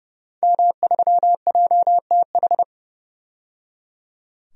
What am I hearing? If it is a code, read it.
Morse code, heard as a keyed tone, about 30 wpm, 700 Hz. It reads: M3JTH
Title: DXCC-100 Call Signs - 5 Characters - 30wpm